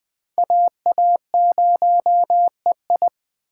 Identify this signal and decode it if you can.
Morse code, heard as a keyed tone, about 21 wpm, 700 Hz. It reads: AA0EI
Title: Speed Racing - USA 2x2 Call Signs 17wpm